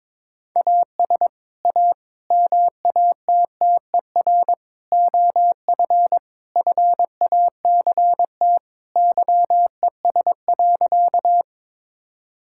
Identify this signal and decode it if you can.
Morse code, heard as a keyed tone, about 21 wpm, 700 Hz. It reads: AS A MATTER OF FACT YES.
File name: practice / 22wpm / Sentences with 4-Word Phrases - 22wpm